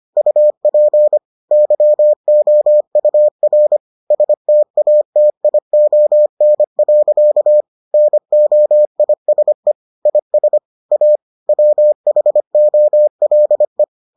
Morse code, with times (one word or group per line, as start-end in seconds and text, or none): 0.2-1.2 UP
1.5-3.8 YOUR
4.1-7.6 STATION.
7.9-9.7 NOISE
10.1-10.6 IS
10.9-11.2 A
11.5-13.8 WHOLE